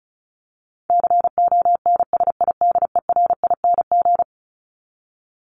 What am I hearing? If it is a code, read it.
Morse code, heard as a keyed tone, about 34 wpm, 700 Hz. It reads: CONSIDERING
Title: ING Suffix - Top 100 Words with Base Words 35wpm